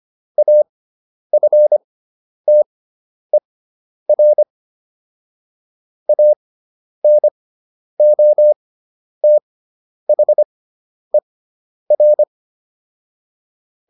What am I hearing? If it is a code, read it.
Morse code, heard as a keyed tone, about 25 wpm, spaced as for 10 wpm, 600 Hz. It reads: AFTER ANOTHER